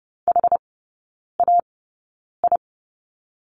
Morse code, heard as a keyed tone, about 30 wpm, spaced as for 9 wpm, 700 Hz. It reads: HAI
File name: course / 30:10wpm / 087 Lesson 15e - Number 5 - Three Characters 30wpm